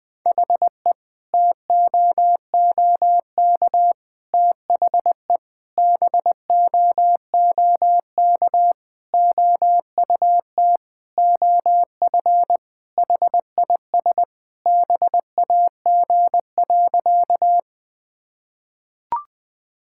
Morse code, heard as a keyed tone, about 20 wpm, 700 Hz. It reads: HE TOOK THE BOOK OUT OF HIS BAG.
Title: Sentences with 3-Word Phrases - Spoken Beginning - 20wpm